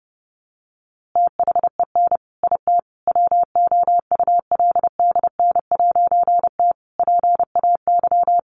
Morse code, heard as a keyed tone, about 28 wpm, 700 Hz. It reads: THEN IT WOULDN'T PAY
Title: A Sound Of Thunder 30wpm